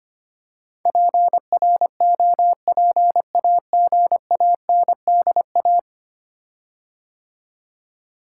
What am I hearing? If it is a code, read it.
Morse code, heard as a keyed tone, about 25 wpm, 700 Hz. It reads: PROPAGANDA
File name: 10 Letter Words - First 3 Letters Given - 25wpm